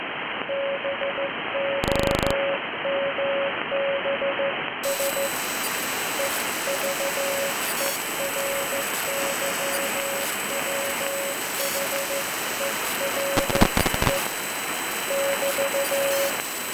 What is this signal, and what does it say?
Morse code, heard as a keyed tone, about 14 wpm, 550 Hz, with noise. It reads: BOMBS EVERYWHERE <BT>